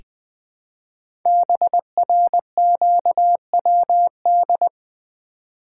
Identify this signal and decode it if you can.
Morse code, heard as a keyed tone, about 20 wpm, 700 Hz. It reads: BRQWD
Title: Code Groups - 5 Letters - 20wpm